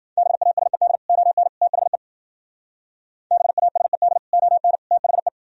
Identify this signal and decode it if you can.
Morse code, heard as a keyed tone, about 51 wpm, 700 Hz. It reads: BASED ON THE BASED ON THE